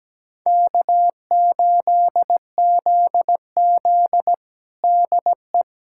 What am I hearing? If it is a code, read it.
Morse code, heard as a keyed tone, about 17 wpm, 700 Hz. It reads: K8ZZ DE
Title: Practice QSO - Exchange 4 - Ending - 17wpm